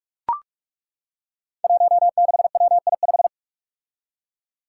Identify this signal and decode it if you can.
Morse code, heard as a keyed tone, about 41 wpm, 700 Hz. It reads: 16WI5
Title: Five Challenging Letter-Number Combinations 45wpm